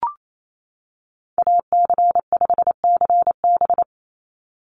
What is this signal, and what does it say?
Morse code, heard as a keyed tone, about 28 wpm, 700 Hz. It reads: AC5CB
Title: USA Call Signs 28wpm